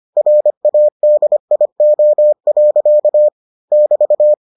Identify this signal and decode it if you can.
Morse code, heard as a keyed tone, about 25 wpm, 600 Hz. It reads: RADIO. <BT>